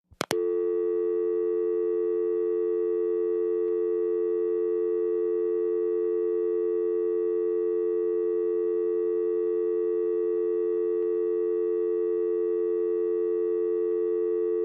Dial tone (North America), 350 + 440 Hz.
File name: functions / downloadSound